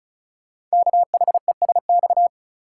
Morse code, heard as a keyed tone, about 33 wpm, 700 Hz. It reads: KHESX
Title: Code Groups - 5 Letters - 35wpm